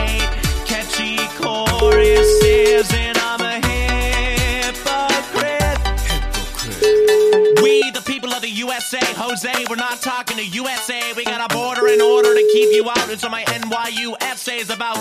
Ringback tone (Europe, 425 Hz), with noise.